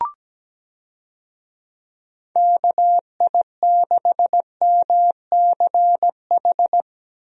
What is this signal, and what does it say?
Morse code, heard as a keyed tone, about 17 wpm, 700 Hz. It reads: KI6MCH
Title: USA Call Signs - 6 Characters - Repeated 2x - 17wpm